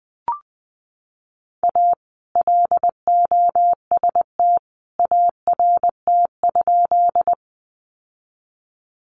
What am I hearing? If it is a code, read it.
Morse code, heard as a keyed tone, about 19 wpm, 700 Hz. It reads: A LOST ART?
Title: Sentences from Top 1400 Words 20wpm